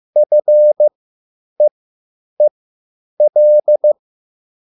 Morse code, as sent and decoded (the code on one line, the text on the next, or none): ..-. . .-..
FEEL